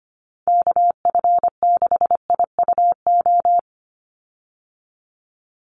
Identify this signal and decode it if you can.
Morse code, heard as a keyed tone, about 25 wpm, 700 Hz. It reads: KF6IUO
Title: USA Call Signs 25wpm